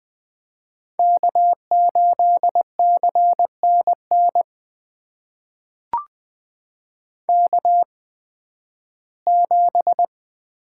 Morse code, heard as a keyed tone, about 20 wpm, spaced as for 13 wpm, 700 Hz. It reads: K8CNN K7
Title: USA Call Signs - 5 Characters - 8x Character Spacing - 20wpm